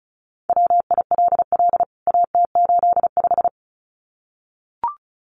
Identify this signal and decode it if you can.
Morse code, heard as a keyed tone, about 34 wpm, 700 Hz. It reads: WILL AT85
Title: CWT Exchange 35wpm